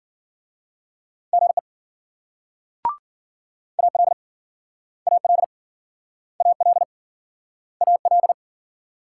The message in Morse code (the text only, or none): GE AL AL AL AL